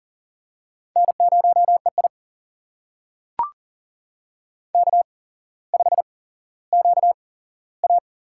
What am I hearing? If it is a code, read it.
Morse code, heard as a keyed tone, about 40 wpm, spaced as for 17 wpm, 700 Hz. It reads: N0EI K5QA
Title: USA Call Signs - 4 Characters - 8x Character Spacing - 40wpm